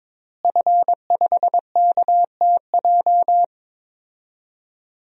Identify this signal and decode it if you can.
Morse code, heard as a keyed tone, about 22 wpm, 700 Hz. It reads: F5KTJ
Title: DXCC-100 Call Signs - 5 Characters - Repeated 3x - 22wpm